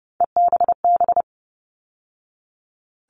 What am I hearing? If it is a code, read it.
Morse code, heard as a keyed tone, about 30 wpm, 700 Hz. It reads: EBB
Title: All Cut Numbers - 3-Digit 30wpm